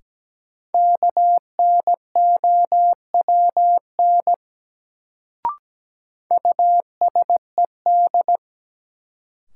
Morse code, heard as a keyed tone, about 17 wpm, 700 Hz. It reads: KNOWN USED
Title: Sentences from Top 600 Words - New Word Drill - 17wpm